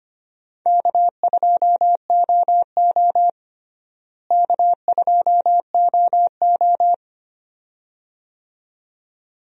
Morse code, heard as a keyed tone, about 25 wpm, 700 Hz. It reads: K2OO K2OO
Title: USA Call Signs - 4 Characters - Repeated 2x - 25wpm